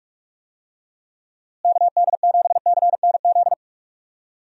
Morse code, heard as a keyed tone, about 42 wpm, 700 Hz. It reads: KD7CNZ
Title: USA Call Signs - 6 Characters - Repeated 2x - 45wpm